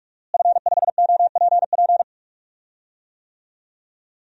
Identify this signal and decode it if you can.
Morse code, heard as a keyed tone, about 46 wpm, 700 Hz. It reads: F5OPP
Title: DXCC-100 Call Signs - 5 Characters - Repeated 2x - 45wpm